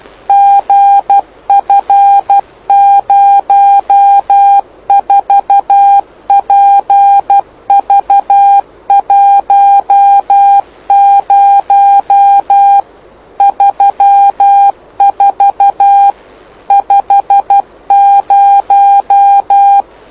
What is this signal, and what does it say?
Morse code, heard as a keyed tone, about 12 wpm, 800 Hz. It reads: GF04PV10 34 50
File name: lu4aao / lu7aa 09dic2011 exp planeador vuelo1